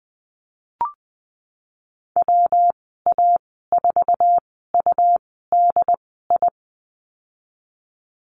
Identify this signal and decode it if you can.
Morse code, heard as a keyed tone, about 19 wpm, spaced as for 14 wpm, 700 Hz. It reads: WA4UDI
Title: 193 Lesson 31g - Letter v - Call Signs 20wpm